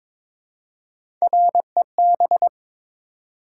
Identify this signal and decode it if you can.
Morse code, heard as a keyed tone, about 22 wpm, 700 Hz. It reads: REB
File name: practice / 22wpm / Three Letter-Number Combinations - 22wpm